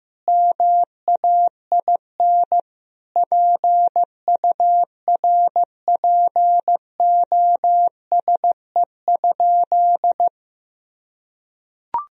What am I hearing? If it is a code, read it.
Morse code, heard as a keyed tone, about 15 wpm, 700 Hz. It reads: MAIN PURPOSE?
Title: Sentences from Top 1000 Words - Review Part 2 - 15wpm